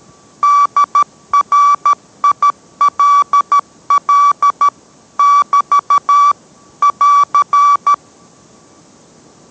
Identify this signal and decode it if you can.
Morse code, heard as a keyed tone, about 13 wpm, 1150 Hz. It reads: DRILL <BT> <AR>